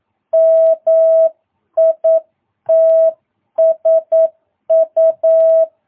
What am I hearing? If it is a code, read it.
Morse code, heard as a keyed tone, about 9 wpm, 650 Hz. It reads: MITSU
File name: jh8cbh / 2022cw